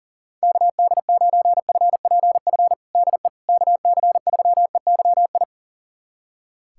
Morse code, heard as a keyed tone, about 37 wpm, 700 Hz. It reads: KD9FPF DE KC3EYI
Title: USA Callsign DE Callsign - 6 Characters - 40wpm